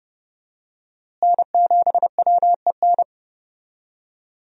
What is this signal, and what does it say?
Morse code, heard as a keyed tone, about 30 wpm, 700 Hz. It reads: N7WEN
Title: USA Call Signs - 5 Characters - Repeated 3x - 30wpm